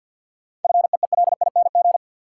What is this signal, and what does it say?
Morse code, heard as a keyed tone, about 50 wpm, 700 Hz. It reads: FEELING
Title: Sentences from Top 1000 Words - Review Part 3 - New Word Drill - 50wpm